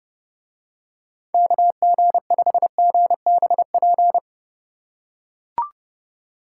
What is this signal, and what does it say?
Morse code, heard as a keyed tone, about 31 wpm, 700 Hz. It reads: KG5GBP